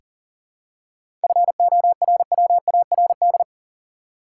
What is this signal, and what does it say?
Morse code, heard as a keyed tone, about 37 wpm, 700 Hz. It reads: FORWARD
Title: Seven Letter Words - 40wpm